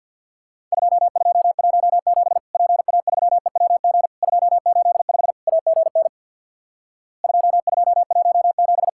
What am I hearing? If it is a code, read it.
Morse code, heard as a keyed tone, about 51 wpm, 700 Hz. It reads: 2217 PA3EWG 285 AGN 2217
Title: Worked All Europe DX Contest - QTC Exchange with Repeats - 50wpm